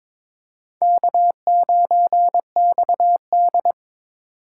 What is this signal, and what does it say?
Morse code, heard as a keyed tone, about 22 wpm, 700 Hz. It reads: K9XD
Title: K1USN SST QSO Practice - 22wpm